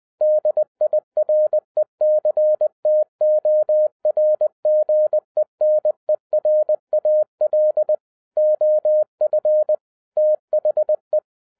Morse code, heard as a keyed tone, about 20 wpm, 610 Hz. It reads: DIRECTOR GENERAL OF THE